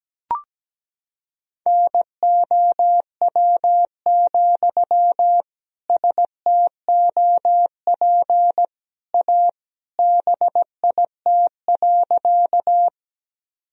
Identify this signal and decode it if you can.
Morse code, heard as a keyed tone, about 17 wpm, 700 Hz. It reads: NOW, STOP A BIT.
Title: Sentences from Top 1000 Words - Review Part 1 - 17wpm